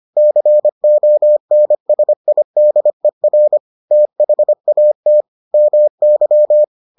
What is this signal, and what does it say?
Morse code, heard as a keyed tone, about 25 wpm, 600 Hz. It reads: CONSIDER THAT MY